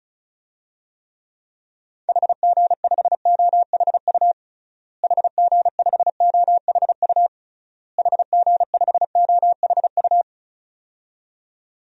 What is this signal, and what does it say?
Morse code, heard as a keyed tone, about 36 wpm, 700 Hz. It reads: HG5OHU HG5OHU HG5OHU
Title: DXCC-100 Call Signs - 6 Characters - Repeated 3x - 35wpm